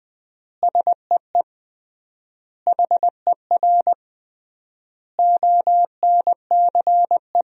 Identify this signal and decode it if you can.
Morse code, heard as a keyed tone, about 20 wpm, 700 Hz. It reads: SEE HER ONCE
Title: 3x Word-Spacing - Sentences from Top 500 Words - 20wpm